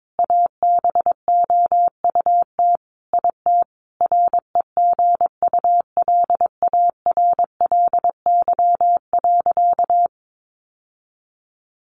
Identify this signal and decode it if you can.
Morse code, heard as a keyed tone, about 22 wpm, 700 Hz. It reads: ABOUT IT REGULARLY.
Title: Sentences from Top 1400 Words 22wpm